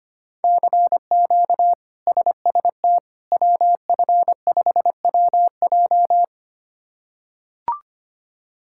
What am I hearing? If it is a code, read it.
Morse code, heard as a keyed tone, about 25 wpm, 700 Hz. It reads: CQ SST WF5WJ
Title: K1USN SST QSO Practice - 25wpm